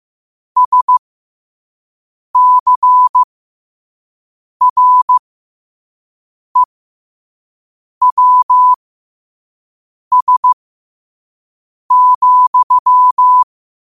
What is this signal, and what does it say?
Morse code, heard as a keyed tone, about 15 wpm, spaced as for 5 wpm, 1000 Hz. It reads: SCREWS,
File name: qotdfiles / 5wpm